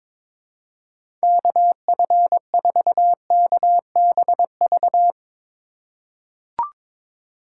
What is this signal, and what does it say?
Morse code, heard as a keyed tone, about 22 wpm, 700 Hz. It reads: KF4KBV
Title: USA Call Signs - 6 Characters - 4x Character Spacing - 22wpm